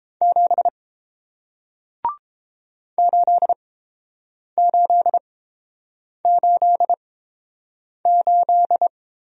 Morse code, heard as a keyed tone, about 30 wpm, spaced as for 7 wpm, 700 Hz. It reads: 7 8888